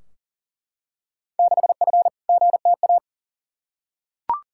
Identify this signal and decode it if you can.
Morse code, heard as a keyed tone, about 40 wpm, 700 Hz. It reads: 6F GTA
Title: Field Day Exchanges Encore - 40wpm